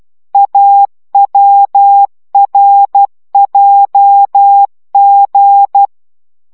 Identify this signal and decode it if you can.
Morse code, heard as a keyed tone, about 12 wpm, 800 Hz. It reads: AWRJG